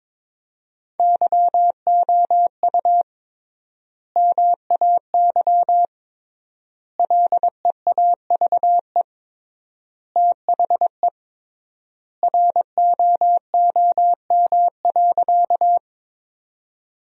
Morse code, heard as a keyed tone, about 22 wpm, 700 Hz. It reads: YOU MAY LEAVE THE ROOM.
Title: 3x Word-Spacing - Sentences from Top 400 Words - 22wpm